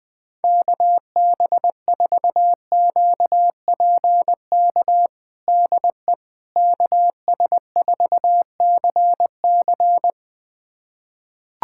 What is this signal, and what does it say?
Morse code, heard as a keyed tone, about 21 wpm, 700 Hz. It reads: KB4QPK DE KS4CC
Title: USA Callsign DE Callsign 20wpm